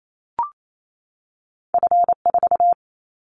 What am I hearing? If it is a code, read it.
Morse code, heard as a keyed tone, about 28 wpm, 700 Hz. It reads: F4
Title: Callsign Prefixes for Top 50 DXCC - 28wpm